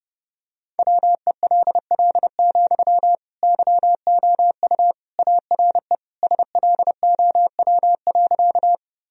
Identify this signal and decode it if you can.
Morse code, heard as a keyed tone, about 32 wpm, 700 Hz. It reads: WELL, YOU ARE SLOW.
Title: Sentences from Top 1000 Words - Review Part 2 - 30wpm